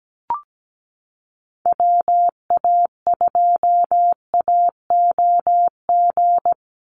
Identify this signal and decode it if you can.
Morse code, heard as a keyed tone, about 17 wpm, 700 Hz. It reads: WA2AOG
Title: Call Signs - 17wpm